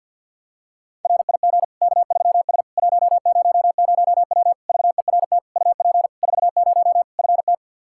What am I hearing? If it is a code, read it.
Morse code, heard as a keyed tone, about 50 wpm, 700 Hz. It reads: RIG K3S 100W VERT UP 40 FT
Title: K1USN SST QSO Practice 50wpm